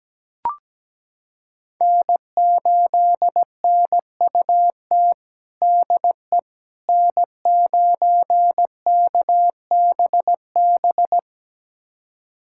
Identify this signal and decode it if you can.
Morse code, heard as a keyed tone, about 17 wpm, 700 Hz. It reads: N8NUT DE N9KBB
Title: USA Callsign DE Callsign - 5 Characters - 17wpm